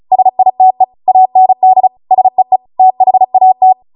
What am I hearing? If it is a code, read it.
Morse code, heard as a keyed tone, about 36 wpm, 750 Hz. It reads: SITE AND SEE THAT
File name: Morse / 35WPM